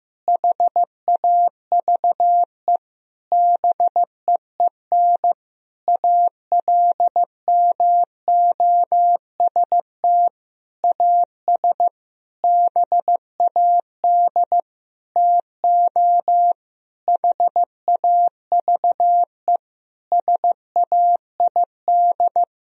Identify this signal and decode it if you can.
Morse code, heard as a keyed tone, about 15 wpm, 700 Hz. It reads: HAVE BEEN ALMOST AS BAD TO HAVE SAID
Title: Sentences from Top 1000 Words - Review Part 4 - 15wpm